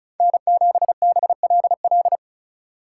Morse code, heard as a keyed tone, about 34 wpm, 700 Hz. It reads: N7BLL
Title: USA Call Signs - 5 Characters - 3x Character Spacing - 35wpm